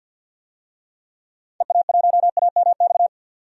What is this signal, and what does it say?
Morse code, heard as a keyed tone, about 51 wpm, 700 Hz. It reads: EA1AMX
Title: DXCC-100 Call Signs - 6 Characters - 50wpm